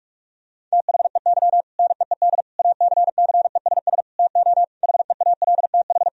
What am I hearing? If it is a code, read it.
Morse code, heard as a keyed tone, about 41 wpm, 700 Hz. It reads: THEY NEED ACCESS TO HEALTH
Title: Sentences from Top 1000 Words - Review Part 4 - 45wpm